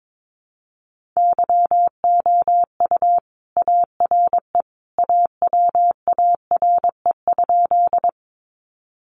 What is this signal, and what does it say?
Morse code, heard as a keyed tone, about 22 wpm, 700 Hz. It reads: YOU ARE AWARE?
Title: Sentences from Top 1200 Words 22wpm